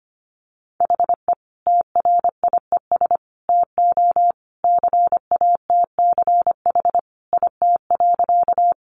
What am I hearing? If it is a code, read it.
Morse code, heard as a keyed tone, about 25 wpm, 700 Hz. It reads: HE TRIES TO CATCH IT.
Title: Additional Easy Phrases 25wpm